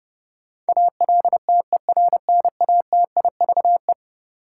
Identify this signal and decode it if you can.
Morse code, heard as a keyed tone, about 30 wpm, 700 Hz. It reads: ALTERNATIVE
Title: Sentences from Top 1000 Words - Review Part 4 - New Word Drill - 30wpm